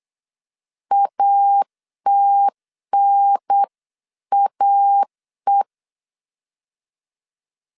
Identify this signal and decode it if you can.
Morse code, heard as a keyed tone, about 8 wpm, 800 Hz. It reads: ATNAE